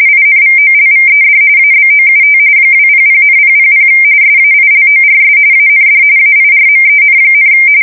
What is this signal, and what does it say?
RTTY, about 45.45 baud. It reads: INVOLVED ABOUT 200,000 SAILORS AND AIRMEN,
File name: fsk170 / battleshipnj